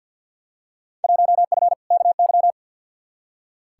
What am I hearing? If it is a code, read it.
Morse code, heard as a keyed tone, about 46 wpm, 700 Hz. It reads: 1F KY
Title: Field Day Exchanges Encore - 50wpm